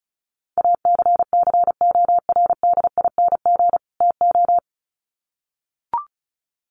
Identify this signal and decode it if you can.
Morse code, heard as a keyed tone, about 34 wpm, 700 Hz. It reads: ACCORDING TO
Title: Sets of 2 Words 35wpm